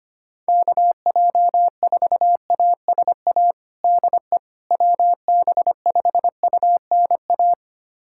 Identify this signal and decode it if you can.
Morse code, heard as a keyed tone, about 25 wpm, 700 Hz. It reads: KJ4ASA DE WB5UNA
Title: USA Callsign DE Callsign - 6 Characters - 25wpm